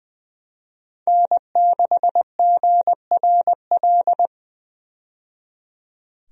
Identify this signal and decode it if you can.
Morse code, heard as a keyed tone, about 20 wpm, 700 Hz. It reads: N6GRL
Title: USA Call Signs - 5 Characters - 20wpm